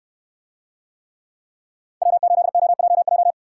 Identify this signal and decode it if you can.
Morse code, heard as a keyed tone, about 69 wpm, 700 Hz. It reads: W8OJJ